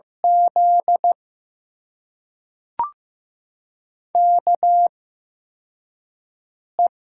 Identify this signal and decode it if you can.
Morse code, heard as a keyed tone, about 15 wpm, spaced as for 4 wpm, 700 Hz. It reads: Z KE
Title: USA Call Signs - 6 Characters - 8x Character Spacing - 15wpm